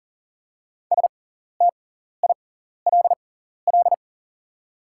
Morse code, heard as a keyed tone, about 40 wpm, spaced as for 14 wpm, 700 Hz. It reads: STILL
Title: 6x Character Spacing - Top 300 Words - 40wpm